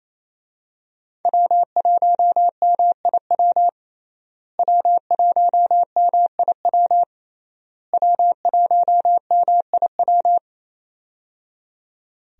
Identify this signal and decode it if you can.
Morse code, heard as a keyed tone, about 27 wpm, 700 Hz. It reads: W1MIW W1MIW W1MIW
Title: USA Call Signs - 5 Characters - Repeated 3x - 28wpm